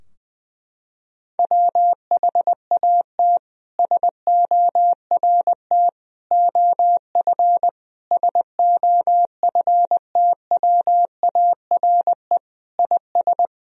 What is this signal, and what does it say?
Morse code, heard as a keyed tone, about 20 wpm, 700 Hz. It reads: WHAT SORT OF SOFTWARE IS
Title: Sentences from Top 1000 Words - Review Part 3 - 20wpm